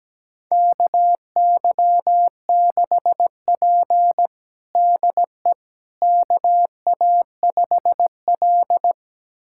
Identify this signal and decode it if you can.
Morse code, heard as a keyed tone, about 17 wpm, 700 Hz. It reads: KY6P DE KA5L
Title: USA Callsign DE Callsign - 4 Characters - 17wpm